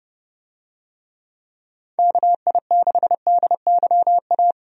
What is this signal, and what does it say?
Morse code, heard as a keyed tone, about 32 wpm, 700 Hz. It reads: KI6DYA
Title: USA Call Signs - 6 Characters - 30wpm